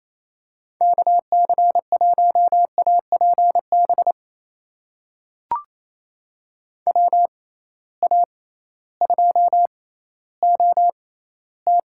Morse code, heard as a keyed tone, about 28 wpm, spaced as for 15 wpm, 700 Hz. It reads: KC1APB WA2OT
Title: USA Call Signs - 6 Characters - 6x Character Spacing - 28wpm